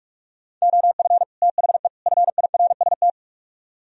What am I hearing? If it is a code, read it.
Morse code, heard as a keyed tone, about 44 wpm, 700 Hz. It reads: OF THE FIRST